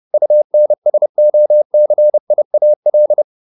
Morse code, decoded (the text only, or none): UNSOCIAL